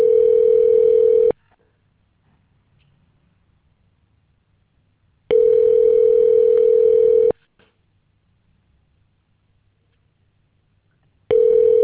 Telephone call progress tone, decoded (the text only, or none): ringback tone